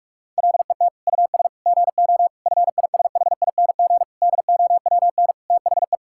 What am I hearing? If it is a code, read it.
Morse code, heard as a keyed tone, about 46 wpm, 700 Hz. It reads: LET US GO FISHING DOWN THE
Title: Sentences from Top 1000 Words - Review Part 5 - 45wpm